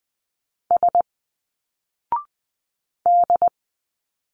Morse code, heard as a keyed tone, about 20 wpm, 700 Hz. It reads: S D
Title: Single Letters 20wpm